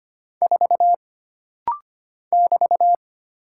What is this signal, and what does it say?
Morse code, heard as a keyed tone, about 25 wpm, 700 Hz. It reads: 4 <BT>
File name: Single Character (Letter:Number:Punctuation:Prosign) - Rapid-Fire - 25wpm